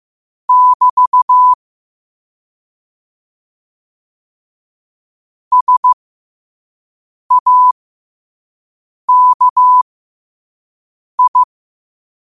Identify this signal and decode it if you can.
Morse code, heard as a keyed tone, about 15 wpm, spaced as for 5 wpm, 1000 Hz. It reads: <BT> SAKI
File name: qotdfiles / 5wpm